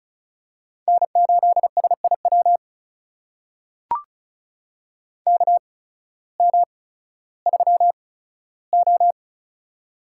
Morse code, heard as a keyed tone, about 36 wpm, spaced as for 16 wpm, 700 Hz. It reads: N8SIW KM3O
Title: USA Call Signs - 8x Character Spacing - 35wpm